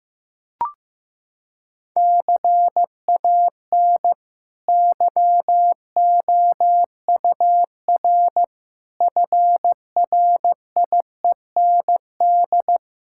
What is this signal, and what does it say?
Morse code, heard as a keyed tone, about 16 wpm, 700 Hz. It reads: CAN YOUR FRIEND
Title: Sentences from Top 1000 Words - Review Part 5 - 15wpm